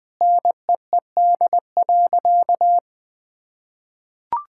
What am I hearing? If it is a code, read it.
Morse code, heard as a keyed tone, about 21 wpm, 700 Hz. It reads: NEED.
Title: Sentences with 5-Word Phrases - 20wpm